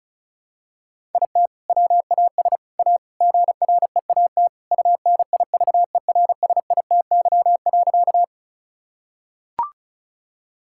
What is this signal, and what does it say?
Morse code, heard as a keyed tone, about 34 wpm, 700 Hz. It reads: IT WAS A GREAT UNIVERSITY.
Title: Sentences from Top 1000 Words - Review Part 5 - 35wpm